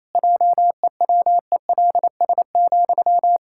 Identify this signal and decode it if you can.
Morse code, heard as a keyed tone, about 28 wpm, 700 Hz. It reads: JEWELS,
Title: Pangram Challenge v2 - 28wpm